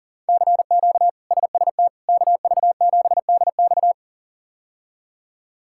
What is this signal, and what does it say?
Morse code, heard as a keyed tone, about 40 wpm, 700 Hz. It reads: CQ SST KV7DX
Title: K1USN SST QSO Practice - 40wpm